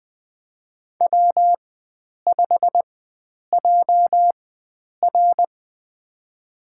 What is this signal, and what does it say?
Morse code, heard as a keyed tone, about 20 wpm, spaced as for 9 wpm, 700 Hz. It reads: W5JR
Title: USA Call Signs - 4 Characters - 4x Character Spacing - 20wpm